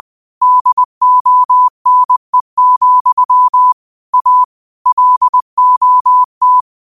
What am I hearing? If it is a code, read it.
Morse code, heard as a keyed tone, about 20 wpm, 1000 Hz. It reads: DONE, A LOT
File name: qotdfiles / 20wpm